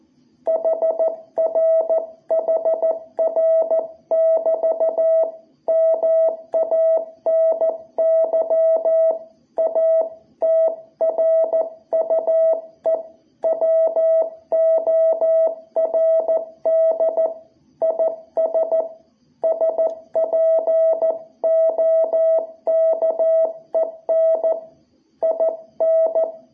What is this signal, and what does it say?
Morse code, heard as a keyed tone, about 14 wpm, 660 Hz. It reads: HR HR <BT> MANY A TRUE WORD IS SPOKEN IN
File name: jh8cbh / 2021cw